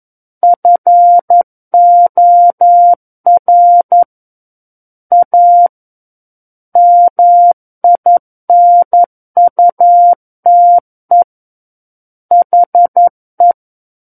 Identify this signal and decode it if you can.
Morse code, heard as a keyed tone, about 11 wpm, 700 Hz. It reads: FOR A MINUTE HE
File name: 11 WPM